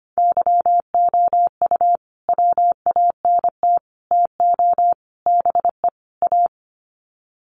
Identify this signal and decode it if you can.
Morse code, heard as a keyed tone, about 25 wpm, 700 Hz. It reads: YOU WANT TO BE A
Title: Sets of 5 Words Encore2 25wpm